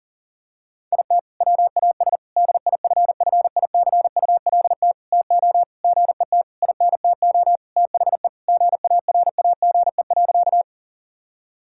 Morse code, heard as a keyed tone, about 40 wpm, 700 Hz. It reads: IT WAS DIFFICULT TO GET INTO THE GARAGE.
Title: Sentences from Top 1600 Words 40wpm